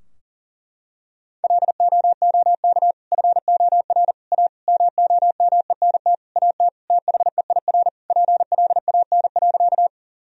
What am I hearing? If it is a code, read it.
Morse code, heard as a keyed tone, about 37 wpm, 700 Hz. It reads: LOOK FOR A MOMENT AT THEIR PLAN.
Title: Sentences from Top 1000 Words - Review Part 2 - 40wpm